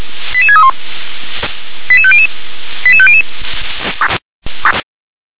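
Five-tone selective call (ZVEI), heard as quick runs of five tones, tones of about 70 ms. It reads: 88631 88500 88500
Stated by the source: Received on an SDR.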